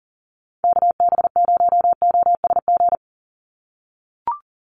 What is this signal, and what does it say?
Morse code, heard as a keyed tone, about 39 wpm, 690 Hz. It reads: KB0OSG
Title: USA Call Signs 40wpm